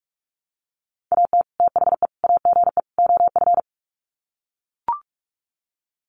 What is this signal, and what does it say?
Morse code, heard as a keyed tone, about 44 wpm, 700 Hz. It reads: AT THE AGE OF
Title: Sets of 4 Words 45wpm